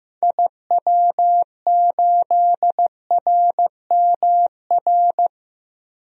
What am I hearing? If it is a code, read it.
Morse code, heard as a keyed tone, about 15 wpm, 700 Hz. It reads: IW8RMR
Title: DXCC-100 Call Signs - 6 Characters - Repeated 3x - 15wpm